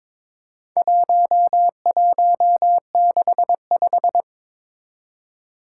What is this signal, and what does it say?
Morse code, heard as a keyed tone, about 23 wpm, 700 Hz. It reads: 1165
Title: Four Challenging Letter-Number Combinations 22wpm